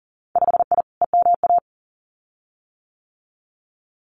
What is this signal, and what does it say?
Morse code, heard as a keyed tone, about 41 wpm, 700 Hz. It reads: 5I EMA
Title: Winter Field Day Exchanges Encore 40wpm